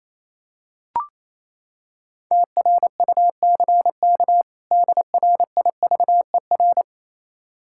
Morse code, heard as a keyed tone, about 27 wpm, 700 Hz. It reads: TRUCK DRIVER
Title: K1USN SST QSO Practice 28wpm